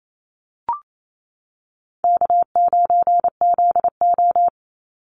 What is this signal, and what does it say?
Morse code, heard as a keyed tone, about 28 wpm, 700 Hz. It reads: K9ZO
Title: Call Signs - 28wpm